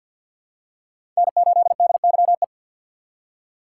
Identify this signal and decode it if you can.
Morse code, heard as a keyed tone, about 54 wpm, 700 Hz. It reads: N8DYE
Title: USA Call Signs - 5 Characters - 6x Character Spacing - 50wpm